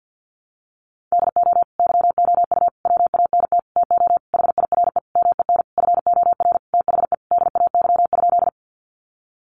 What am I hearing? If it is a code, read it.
Morse code, heard as a keyed tone, about 50 wpm, 700 Hz. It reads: DO YOU WANT TO HIRE MEN FOR THE DAY?
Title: Sentences from Top 1400 Words 50wpm